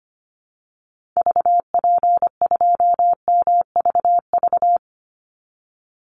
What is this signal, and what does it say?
Morse code, heard as a keyed tone, about 25 wpm, 700 Hz. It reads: VP2MVV
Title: Call Signs Repeated 2x - 25wpm